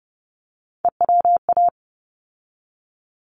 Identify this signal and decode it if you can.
Morse code, heard as a keyed tone, about 30 wpm, 700 Hz. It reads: EWA
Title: ARRL Sections 30wpm